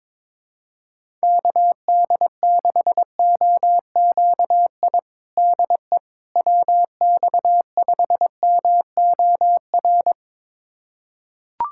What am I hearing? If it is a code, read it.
Morse code, heard as a keyed tone, about 22 wpm, 700 Hz. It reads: KD6OQI DE WX5MOR
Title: USA Callsign DE Callsign - 6 Characters - 22wpm